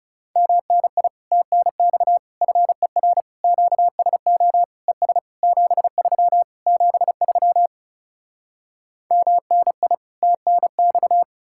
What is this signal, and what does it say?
Morse code, heard as a keyed tone, about 40 wpm, 700 Hz. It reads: MNI TNX FER QSO ES 73 73 MNI TNX